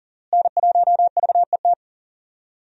Morse code, heard as a keyed tone, about 40 wpm, 700 Hz. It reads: N1VET
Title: USA Call Signs - 5 Characters - Repeated 3x - 40wpm